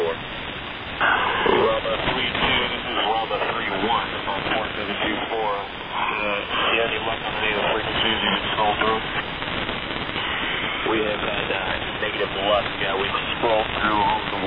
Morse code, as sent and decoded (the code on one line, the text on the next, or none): none